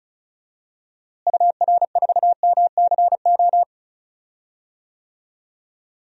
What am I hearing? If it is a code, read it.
Morse code, heard as a keyed tone, about 34 wpm, 700 Hz. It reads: UR4MCO
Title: DXCC-100 Call Signs - 6 Characters - Repeated 2x - 35wpm